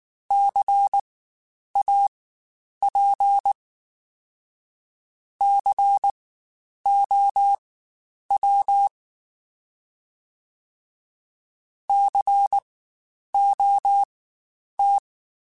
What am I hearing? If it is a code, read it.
Morse code, heard as a keyed tone, about 19 wpm, spaced as for 9 wpm, 800 Hz. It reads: CAP COW COT
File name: Spaced Alphas / Spaced Alphas mp3s / Cs sp @18